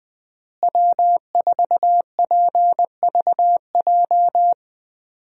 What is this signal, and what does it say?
Morse code, heard as a keyed tone, about 20 wpm, 700 Hz. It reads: W4PVJ
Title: USA Call Signs - 5 Characters - Repeated 2x - 20wpm